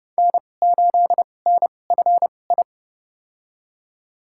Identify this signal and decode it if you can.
Morse code, heard as a keyed tone, about 29 wpm, spaced as for 21 wpm, 700 Hz. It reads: N8NFI
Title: USA Call Signs - 5 Characters - 2x Character Spacing - 30wpm